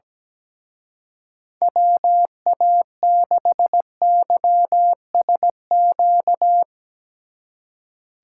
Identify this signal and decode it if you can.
Morse code, heard as a keyed tone, about 17 wpm, 700 Hz. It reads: WA6YSQ
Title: Speed Racing - USA 2x3 Call Signs 17wpm